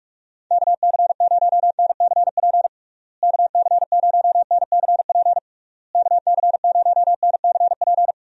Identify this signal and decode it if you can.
Morse code, heard as a keyed tone, about 42 wpm, 700 Hz. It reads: KC0NCP KC0NCP KC0NCP
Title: USA Call Signs - 6 Characters - Repeated 3x - 45wpm